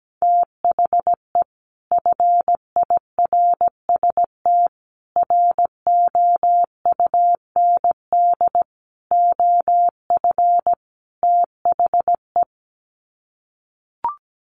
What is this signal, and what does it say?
Morse code, heard as a keyed tone, about 17 wpm, 700 Hz. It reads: THE FIRST ROUND OF THE
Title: Sets of 5 Words Encore 17wpm